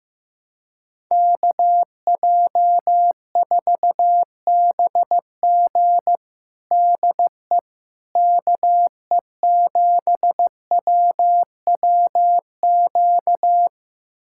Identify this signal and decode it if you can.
Morse code, heard as a keyed tone, about 15 wpm, 700 Hz. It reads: KJ4BG DE KE7WWQ
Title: USA Callsign DE Callsign 15wpm